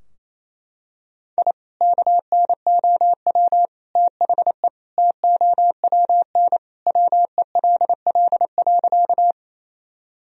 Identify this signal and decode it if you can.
Morse code, heard as a keyed tone, about 28 wpm, 700 Hz. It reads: I KNOW THE TOWN WELL.
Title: Sentences from Top 1000 Words - Review Part 2 - 28wpm